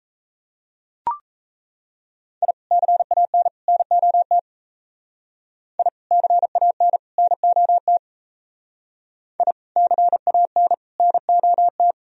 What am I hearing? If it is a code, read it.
Morse code, heard as a keyed tone, about 34 wpm, 700 Hz. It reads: I CAN NOT I CAN NOT I CAN NOT
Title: Speed-Racing - Sets of 3 Words 28wpm